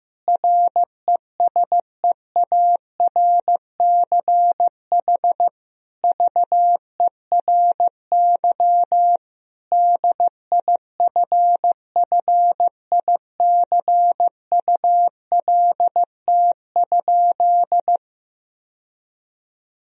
Morse code, heard as a keyed tone, about 15 wpm, 700 Hz. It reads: RESEARCH VERY DIFFICULT?
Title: Sentences from Top 1000 Words - Review Part 4 - 15wpm